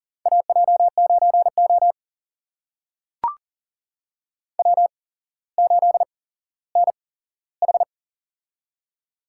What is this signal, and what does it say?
Morse code, heard as a keyed tone, about 38 wpm, spaced as for 17 wpm, 700 Hz. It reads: AJ9O W8NH
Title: USA Call Signs - 4 Characters - 8x Character Spacing - 40wpm